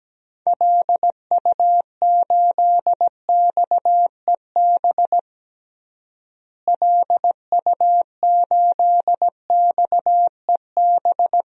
Morse code, heard as a keyed tone, about 17 wpm, 700 Hz. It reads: LU8XEB LU8XEB
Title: DXCC-100 Call Signs - 6 Characters - Repeated 2x - 17wpm